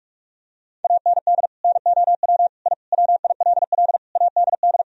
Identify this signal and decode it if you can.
Morse code, heard as a keyed tone, about 46 wpm, 700 Hz. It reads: AND NOW I WILL ADD
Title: Sentences from Top 1000 Words - Review Part 1 - 45wpm